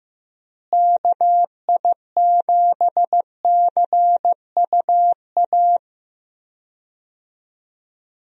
Morse code, heard as a keyed tone, about 15 wpm, 700 Hz. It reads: KI7CUA
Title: USA Call Signs - 6 Characters - Repeated 2x - 15wpm